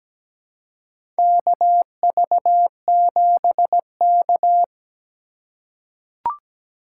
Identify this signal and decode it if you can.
Morse code, heard as a keyed tone, about 17 wpm, 700 Hz. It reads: KV7K
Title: USA Call Signs - 2x Character Spacing - 17wpm